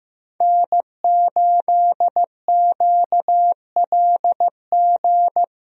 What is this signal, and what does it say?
Morse code, heard as a keyed tone, about 15 wpm, 700 Hz. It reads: N8QLG
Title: USA Call Signs - 5 Characters - Repeated 2x - 15wpm